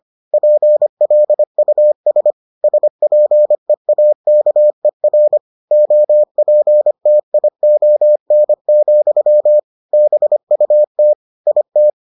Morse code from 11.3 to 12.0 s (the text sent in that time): IT